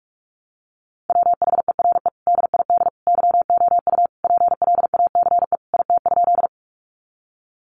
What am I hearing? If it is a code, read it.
Morse code, heard as a keyed tone, about 44 wpm, 700 Hz. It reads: WHERE DID YOU PLACE IT?
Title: Sentences from Top 200 Words 45wpm